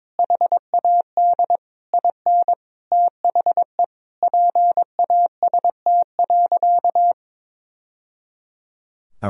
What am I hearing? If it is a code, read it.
Morse code, heard as a keyed tone, about 22 wpm, 700 Hz. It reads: HAD IN THE PAST.
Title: Sentences from Top 1000 Words - Review Part 3 - 22wpm